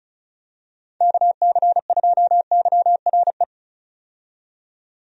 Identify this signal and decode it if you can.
Morse code, heard as a keyed tone, about 36 wpm, 700 Hz. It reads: KC2YRE
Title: USA Call Signs - 6 Characters - 35wpm